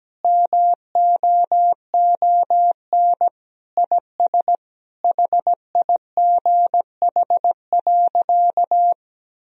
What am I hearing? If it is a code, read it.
Morse code, heard as a keyed tone, about 17 wpm, 700 Hz. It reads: MOON IS HIGH.
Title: Sentences from Top 1000 Words - Review Part 3 - 17wpm